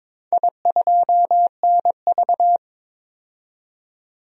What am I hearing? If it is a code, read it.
Morse code, heard as a keyed tone, about 22 wpm, 700 Hz. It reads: I2NV
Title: DXCC-100 Call Signs - 4 Characters - 22wpm